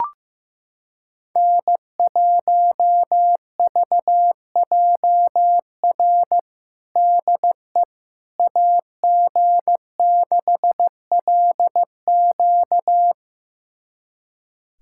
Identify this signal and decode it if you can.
Morse code, heard as a keyed tone, about 15 wpm, 700 Hz. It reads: N1VJR DE AG6LQ
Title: USA Callsign DE Callsign - 5 Characters - 15wpm